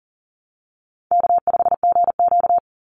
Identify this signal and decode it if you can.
Morse code, heard as a keyed tone, about 40 wpm, 700 Hz. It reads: K5GQ
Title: Easy Call Signs Repeated 3x - 40wpm